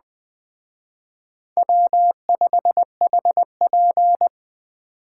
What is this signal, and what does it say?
Morse code, heard as a keyed tone, about 20 wpm, 700 Hz. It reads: W5HP
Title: Speed Racing - USA Call Signs 17wpm